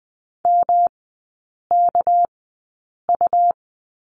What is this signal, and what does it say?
Morse code, heard as a keyed tone, about 20 wpm, spaced as for 8 wpm, 700 Hz. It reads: MKU
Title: 198 Lesson 32e - Letter k - Three Characters 20wpm